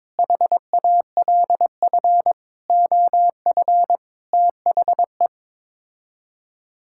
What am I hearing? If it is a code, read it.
Morse code, heard as a keyed tone, about 22 wpm, 700 Hz. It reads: HALF OF THE
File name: practice / 22wpm / Sets of 6 Words - 22wpm